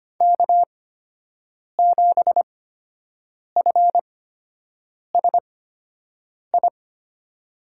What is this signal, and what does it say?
Morse code, heard as a keyed tone, about 25 wpm, spaced as for 7 wpm, 700 Hz. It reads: K7FSI
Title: USA Call Signs - 5 Characters - 8x Character Spacing - 25wpm